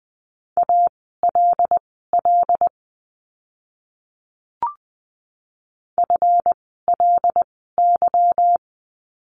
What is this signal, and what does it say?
Morse code, heard as a keyed tone, about 20 wpm, spaced as for 15 wpm, 700 Hz. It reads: ALL FLY
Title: 146 Lesson 24f - Letter y - Words 20wpm